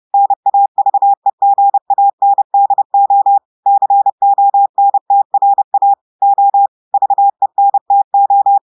Morse code, heard as a keyed tone, about 30 wpm, 800 Hz. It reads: NAVEGANDO CONTRA O VENTO
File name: audios / portugais / portugais texte en clair 1 vitesse 30 mots minutes